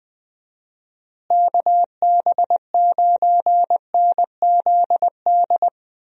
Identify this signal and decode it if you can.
Morse code, heard as a keyed tone, about 20 wpm, 700 Hz. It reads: KB9NZD
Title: USA Call Signs - 6 Characters - Repeated 2x - 20wpm